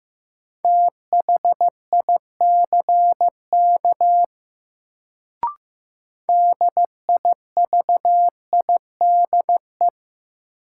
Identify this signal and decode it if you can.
Morse code, heard as a keyed tone, about 15 wpm, 700 Hz. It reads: THICK DIVIDE
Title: Sentences from Top 1600 Words - New Word Drill - 15wpm